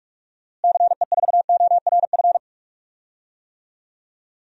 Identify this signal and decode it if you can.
Morse code, heard as a keyed tone, about 46 wpm, 700 Hz. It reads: CE4ORF
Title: DXCC-100 Call Signs - 6 Characters - Repeated 3x - 45wpm